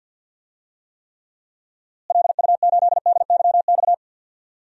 Morse code, heard as a keyed tone, about 52 wpm, 700 Hz. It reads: LU8DYX
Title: DXCC-100 Call Signs - 6 Characters - Repeated 3x - 50wpm